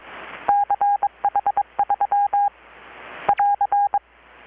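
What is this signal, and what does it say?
Morse code, heard as a keyed tone, about 21 wpm, 810 Hz. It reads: CH3 <AR>
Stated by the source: Morse from KPH's Night of Nights XX maritime radiotelegraphy event, received in Chiapas, Mexico about 3,800 km away on a Kenwood R-600 with a grounded 10 m long-wire antenna.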